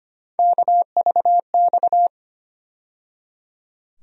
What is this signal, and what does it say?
Morse code, heard as a keyed tone, about 25 wpm, 700 Hz. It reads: KVX
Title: USA Call Sign Postfixes - 3 Letters 25wpm